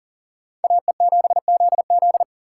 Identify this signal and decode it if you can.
Morse code, heard as a keyed tone, about 51 wpm, 700 Hz. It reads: AE7ZZ